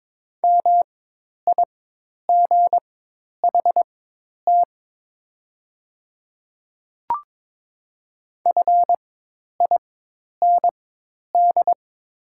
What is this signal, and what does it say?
Morse code, heard as a keyed tone, about 22 wpm, spaced as for 10 wpm, 700 Hz. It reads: MIGHT FIND
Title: 4x Character Spacing - Top 300 Words - 22wpm